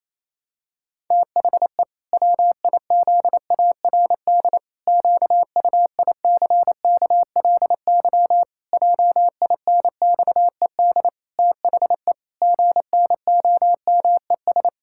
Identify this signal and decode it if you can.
Morse code, heard as a keyed tone, about 28 wpm, 700 Hz. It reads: THE WIZARD QUICKLY JINXED THE GNOMES
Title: Pangram Challenge v2 - 28wpm